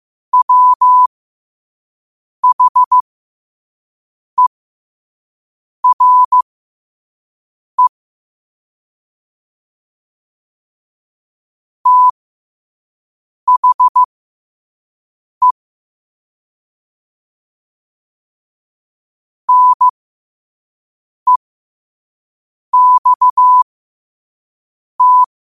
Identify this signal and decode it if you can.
Morse code, heard as a keyed tone, about 15 wpm, spaced as for 5 wpm, 1000 Hz. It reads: WHERE THE NEXT